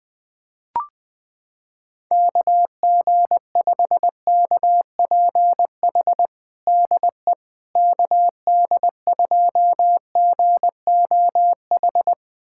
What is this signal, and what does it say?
Morse code, heard as a keyed tone, about 20 wpm, 700 Hz. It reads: KG5KPH DE KD2GOH
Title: USA Callsign DE Callsign 20wpm